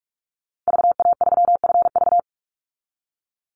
Morse code, heard as a keyed tone, about 44 wpm, 700 Hz. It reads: VA3FV
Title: Call Signs Repeated 2x - 45wpm